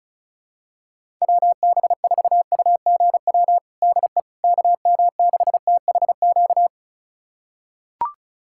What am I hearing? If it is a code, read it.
Morse code, heard as a keyed tone, about 36 wpm, 700 Hz. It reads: WB4UGW DE KM6THQ
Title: USA Callsign DE Callsign - 6 Characters - 35wpm